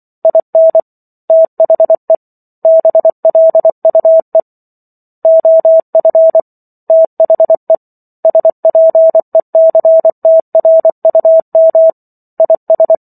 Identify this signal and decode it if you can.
Morse code, heard as a keyed tone, about 24 wpm, 650 Hz. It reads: IN THE BLUE OF THE SPECTRUM IS